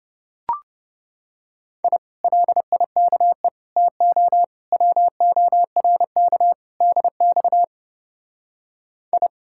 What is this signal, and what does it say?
Morse code, heard as a keyed tone, about 30 wpm, 700 Hz. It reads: I LIKE TO WORK DX I